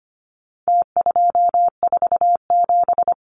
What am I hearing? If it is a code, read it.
Morse code, heard as a keyed tone, about 25 wpm, 700 Hz. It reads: T247
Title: Numbers - Contest Style - 4-Digit 25wpm